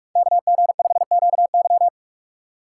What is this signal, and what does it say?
Morse code, heard as a keyed tone, about 46 wpm, 700 Hz. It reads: KG5QY
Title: USA Call Signs - 5 Characters - 4x Character Spacing - 45wpm